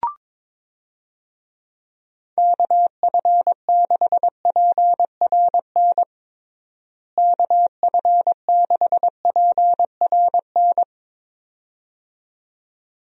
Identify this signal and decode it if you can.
Morse code, heard as a keyed tone, about 22 wpm, 700 Hz. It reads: KF6PRN KF6PRN
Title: USA Call Signs - 6 Characters - Repeated 2x - 22wpm